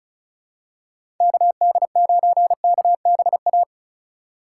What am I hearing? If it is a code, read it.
Morse code, heard as a keyed tone, about 36 wpm, 690 Hz. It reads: KD9KBA
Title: USA Call Signs - 4x Character Spacing - 35wpm